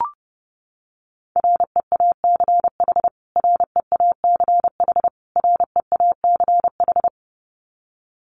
Morse code, heard as a keyed tone, about 30 wpm, 700 Hz. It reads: REACH REACH REACH
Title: Top 500 Words Repeated 30wpm